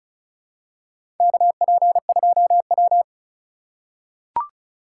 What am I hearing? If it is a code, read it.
Morse code, heard as a keyed tone, about 36 wpm, 700 Hz. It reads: KP2W
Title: USA Call Signs - 4 Characters - 2x Character Spacing - 35wpm